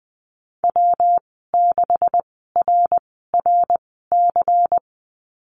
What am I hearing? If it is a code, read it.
Morse code, heard as a keyed tone, about 20 wpm, spaced as for 14 wpm, 700 Hz. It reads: W6RRC
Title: 140 Lesson 23g - Letter f - Call Signs 20wpm